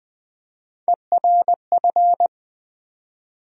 Morse code, heard as a keyed tone, about 20 wpm, 700 Hz. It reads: ERF
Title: USA Call Sign Postfixes - 3 Letters 20wpm